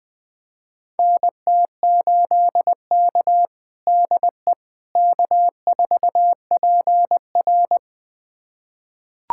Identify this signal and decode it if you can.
Morse code, heard as a keyed tone, about 20 wpm, 700 Hz. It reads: NT8K DE K4PR
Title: USA Callsign DE Callsign - 4 Characters - 20wpm